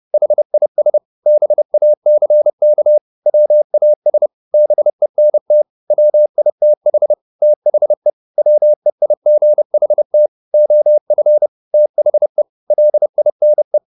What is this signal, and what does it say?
Morse code, heard as a keyed tone, about 30 wpm, 600 Hz. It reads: HIS BACK WAS BENT WITH THE WEIGHT OF THE LINE